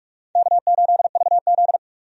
Morse code, heard as a keyed tone, about 46 wpm, 700 Hz. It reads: K8VZ
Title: USA Call Signs - 4 Characters - 45wpm